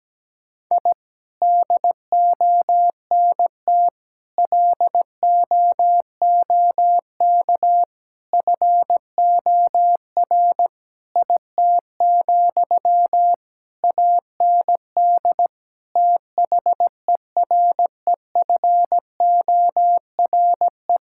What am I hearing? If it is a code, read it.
Morse code, heard as a keyed tone, about 17 wpm, 700 Hz. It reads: I DONT LOOK FOR IT, AND THEREFORE
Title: Sentences from Top 1000 Words - Review Part 5 - 17wpm